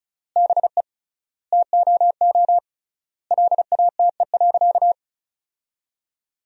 Morse code, heard as a keyed tone, about 36 wpm, 700 Hz. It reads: BE TOO LATE.
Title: 3x Word-Spacing - Sentences from Top 300 Words - 35wpm